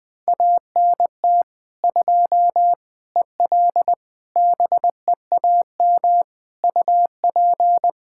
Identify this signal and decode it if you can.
Morse code, heard as a keyed tone, about 20 wpm, 700 Hz. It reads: ANT 2 EL BEAM UP
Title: Practice QSO - Exchange 2 - Info - 20wpm